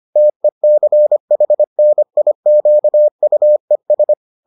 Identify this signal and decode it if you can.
Morse code, heard as a keyed tone, about 25 wpm, 600 Hz. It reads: TECHNIQUES